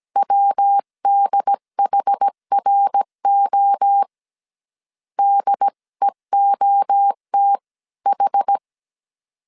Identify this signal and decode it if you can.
Morse code, heard as a keyed tone, about 17 wpm, 800 Hz. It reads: WDHRO DEOTH